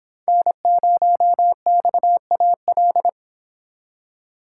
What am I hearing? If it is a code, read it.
Morse code, heard as a keyed tone, about 30 wpm, 700 Hz. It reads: N0XAL